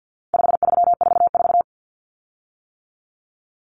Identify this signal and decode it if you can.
Morse code, heard as a keyed tone, about 50 wpm, 700 Hz. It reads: 5344
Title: Common Cut Numbers - 4-Digit 50wpm